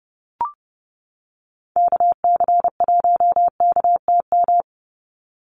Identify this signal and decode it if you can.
Morse code, heard as a keyed tone, about 30 wpm, 700 Hz. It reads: KC1KTM
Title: USA Call Signs 30wpm